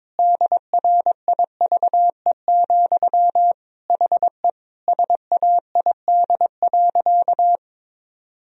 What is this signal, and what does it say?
Morse code, heard as a keyed tone, about 22 wpm, 700 Hz. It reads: DRIVE, HE SAID.
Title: Sentences from Top 1000 Words - Review Part 2 - 22wpm